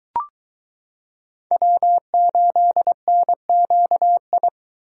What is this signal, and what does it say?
Morse code, heard as a keyed tone, about 22 wpm, 700 Hz. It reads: W8NQI